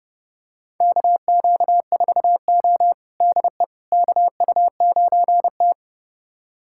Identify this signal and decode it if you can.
Morse code, heard as a keyed tone, about 30 wpm, 700 Hz. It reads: KQ4O DE KU9T
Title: USA Callsign DE Callsign - 4 Characters - 30wpm